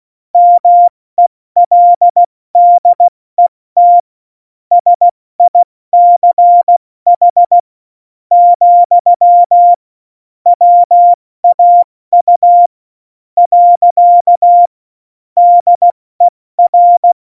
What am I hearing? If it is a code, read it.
Morse code, heard as a keyed tone, about 16 wpm, spaced as for 13 wpm, 700 Hz. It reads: MELDET SICH, WAU. DER
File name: Morsekurs / Witze / Witz 05. Juni 2018, 70 F80